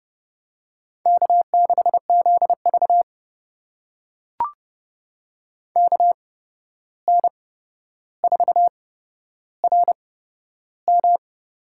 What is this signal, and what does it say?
Morse code, heard as a keyed tone, about 30 wpm, spaced as for 12 wpm, 700 Hz. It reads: K6ZV KN4RM
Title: USA Call Signs - 8x Character Spacing - 30wpm